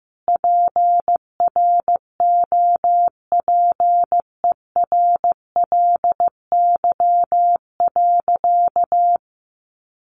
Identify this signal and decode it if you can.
Morse code, heard as a keyed tone, about 15 wpm, 700 Hz. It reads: PROPERLY.
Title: Sentences from Top 1100 Words 15wpm